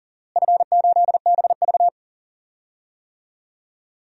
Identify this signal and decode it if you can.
Morse code, heard as a keyed tone, about 40 wpm, 700 Hz. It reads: F8BV
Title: DXCC-100 Call Signs - 4 Characters - Repeated 2x - 40wpm